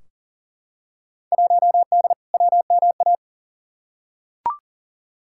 Morse code, heard as a keyed tone, about 40 wpm, 700 Hz. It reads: 1D WMA
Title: Field Day Exchanges - 40wpm